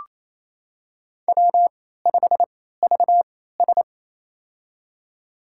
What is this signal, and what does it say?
Morse code, heard as a keyed tone, about 28 wpm, spaced as for 16 wpm, 700 Hz. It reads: W5VS
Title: USA Call Signs - 3x Character Spacing - 28wpm